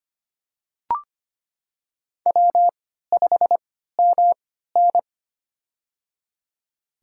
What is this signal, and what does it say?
Morse code, heard as a keyed tone, about 25 wpm, spaced as for 14 wpm, 700 Hz. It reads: W5MN
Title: USA Call Signs - 4 Characters - 3x Character Spacing - 25wpm